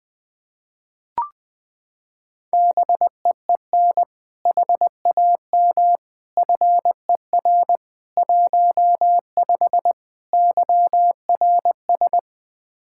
Morse code, heard as a keyed tone, about 20 wpm, 700 Hz. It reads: BEEN HAM FER 15 YRS
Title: Practice QSO - Exchange 3 - More Info - 20wpm